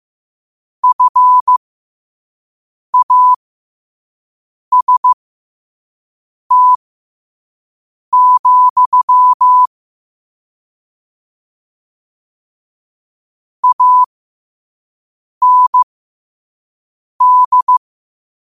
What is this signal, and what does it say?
Morse code, heard as a keyed tone, about 15 wpm, spaced as for 5 wpm, 1000 Hz. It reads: FAST, AND
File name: qotdfiles / 5wpm